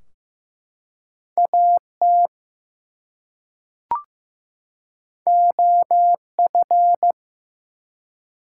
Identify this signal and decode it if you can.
Morse code, heard as a keyed tone, about 15 wpm, 700 Hz. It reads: AT OF
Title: Two Letter Words - 15wpm